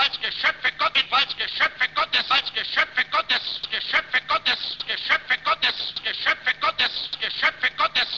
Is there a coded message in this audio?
Yes: Emergency Alert System (SAME) tones (header bursts) from 1.2 to 7.6 s, heard as harsh buzzing bursts.